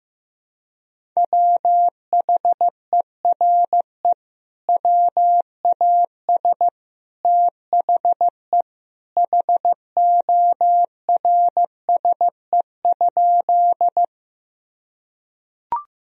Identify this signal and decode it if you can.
Morse code, heard as a keyed tone, about 16 wpm, 700 Hz. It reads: WHERE WAS THE HORSE?
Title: Sentences from Top 1000 Words - Review Part 1 - 15wpm